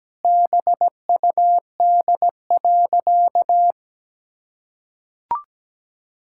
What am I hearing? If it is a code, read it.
Morse code, heard as a keyed tone, about 17 wpm, 700 Hz. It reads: BUD.
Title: Pangram Challenge v2 - 17wpm